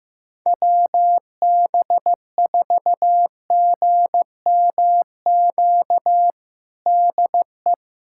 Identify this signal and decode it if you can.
Morse code, heard as a keyed tone, about 15 wpm, 700 Hz. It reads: WB4GMQ DE
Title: USA Callsign DE Callsign - 6 Characters - 15wpm